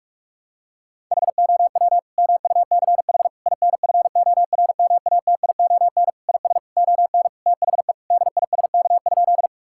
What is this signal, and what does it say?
Morse code, heard as a keyed tone, about 42 wpm, 700 Hz. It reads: HOW MUCH INFORMATION IS ON THE DISK?
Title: Sentences from Top 1000 Words - Review Part 6 - 45wpm